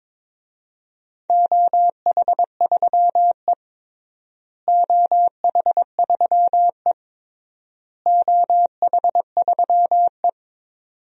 Morse code, heard as a keyed tone, about 22 wpm, 700 Hz. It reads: OH3E OH3E OH3E
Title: DXCC-100 Call Signs - 4 Characters - Repeated 3x - 22wpm